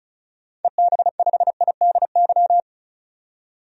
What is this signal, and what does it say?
Morse code, heard as a keyed tone, about 35 wpm, 700 Hz. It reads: EB5IDY
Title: DXCC-100 Call Signs - 6 Characters - 35wpm